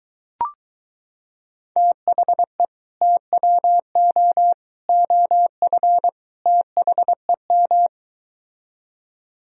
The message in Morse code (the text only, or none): THE TWO OF THEM